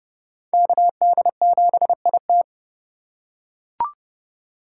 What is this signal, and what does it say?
Morse code, heard as a keyed tone, about 31 wpm, 700 Hz. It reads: KD7IT
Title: USA Call Signs - 5 Characters - 2x Character Spacing - 30wpm